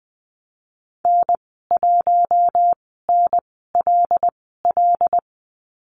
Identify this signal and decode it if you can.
Morse code, heard as a keyed tone, about 20 wpm, spaced as for 15 wpm, 700 Hz. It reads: N1NLL
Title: 088 Lesson 15g - Number 5 - Call Signs 20wpm